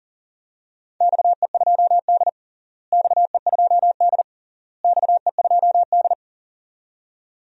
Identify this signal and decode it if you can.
Morse code, heard as a keyed tone, about 40 wpm, 700 Hz. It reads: XE2D XE2D XE2D
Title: DXCC-100 Call Signs - 4 Characters - Repeated 3x - 40wpm